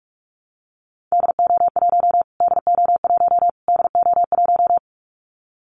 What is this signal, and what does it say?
Morse code, heard as a keyed tone, about 44 wpm, 700 Hz. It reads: DO1 DO1 DO1
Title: Callsign Prefixes for Top 25 DXCC - Repeated 3x - 45wpm